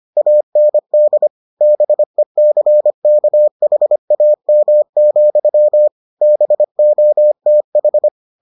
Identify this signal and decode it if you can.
Morse code, heard as a keyed tone, about 25 wpm, 600 Hz. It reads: AND BECKHAM, BOTH